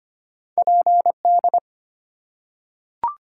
Morse code, heard as a keyed tone, about 25 wpm, 700 Hz. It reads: PD
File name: Two Letter-Number Combinations - 25wpm